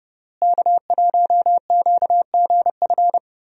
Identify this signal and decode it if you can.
Morse code, heard as a keyed tone, about 30 wpm, 700 Hz. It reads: K1QGF
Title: USA Call Signs - 5 Characters - Repeated 3x - 30wpm